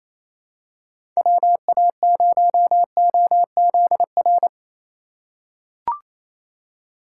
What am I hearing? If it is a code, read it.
Morse code, heard as a keyed tone, about 28 wpm, 700 Hz. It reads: WA0OZR
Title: USA Call Signs - 6 Characters - 4x Character Spacing - 28wpm